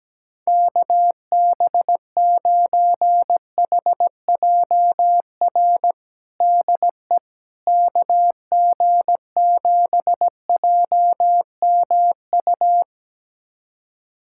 Morse code, heard as a keyed tone, about 17 wpm, 700 Hz. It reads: KB9HJR DE KG7JMU
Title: USA Callsign DE Callsign - 6 Characters - 17wpm